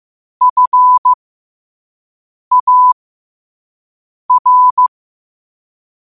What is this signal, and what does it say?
Morse code, heard as a keyed tone, about 15 wpm, spaced as for 5 wpm, 1000 Hz. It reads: FAR
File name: qotdfiles / 5wpm